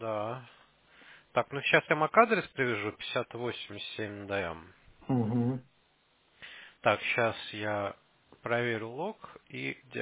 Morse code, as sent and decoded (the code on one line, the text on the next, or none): none